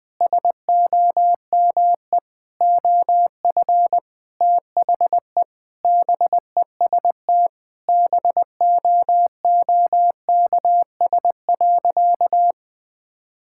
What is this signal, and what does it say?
Morse code, heard as a keyed tone, about 20 wpm, 700 Hz. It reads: SOME OF THE BEST BOOKS.
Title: Sentences with 8-Word Phrases - 20wpm